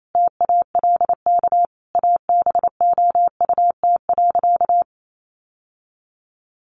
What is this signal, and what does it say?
Morse code, heard as a keyed tone, about 28 wpm, 700 Hz. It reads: TALK ABOUT.
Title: Sentences from Top 800 Words 28wpm